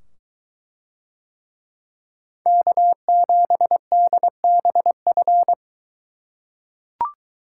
Morse code, heard as a keyed tone, about 22 wpm, 700 Hz. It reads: K7DBF